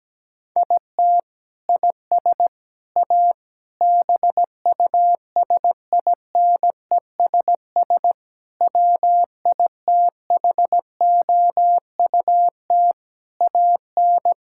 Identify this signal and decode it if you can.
Morse code, heard as a keyed tone, about 17 wpm, 700 Hz. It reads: IT IS A BUSINESS WITHOUT AN
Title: Sentences from Top 1600 Words 17wpm